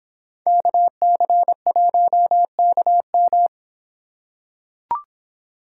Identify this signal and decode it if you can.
Morse code, heard as a keyed tone, about 26 wpm, 700 Hz. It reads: KC1KM